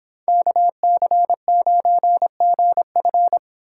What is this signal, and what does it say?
Morse code, heard as a keyed tone, about 24 wpm, 700 Hz. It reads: KC9GF